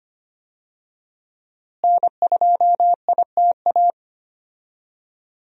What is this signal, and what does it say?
Morse code, heard as a keyed tone, about 25 wpm, 700 Hz. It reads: N2ITA
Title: USA Call Signs - 5 Characters - Repeated 3x - 25wpm